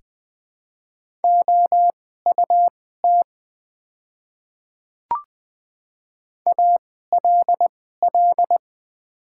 Morse code, heard as a keyed tone, about 20 wpm, spaced as for 14 wpm, 700 Hz. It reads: OUT ALL
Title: 2x Character Spacing - Top 100 Words - 20wpm